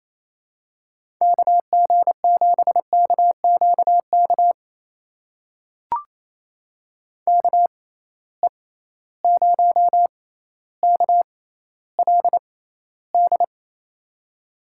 Morse code, heard as a keyed tone, about 28 wpm, spaced as for 14 wpm, 700 Hz. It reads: KG7KQK KE0KLD
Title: USA Call Signs - 6 Characters - 6x Character Spacing - 28wpm